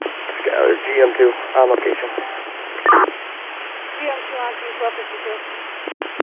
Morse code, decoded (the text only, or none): none